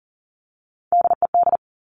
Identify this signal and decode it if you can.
Morse code, heard as a keyed tone, about 42 wpm, 700 Hz. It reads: DED
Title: Top Three Letter Combinations in Words 40wpm